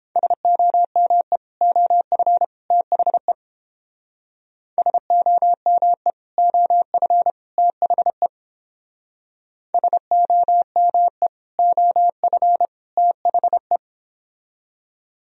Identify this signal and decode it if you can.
Morse code, heard as a keyed tone, about 29 wpm, 700 Hz. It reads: SOME OF THE SOME OF THE SOME OF THE